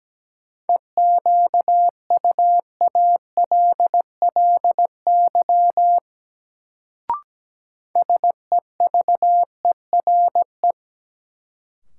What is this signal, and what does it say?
Morse code, heard as a keyed tone, about 17 wpm, 700 Hz. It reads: EQUALLY SEVERE
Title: Sentences from Top 1400 Words - New Word Drill Alt - 17wpm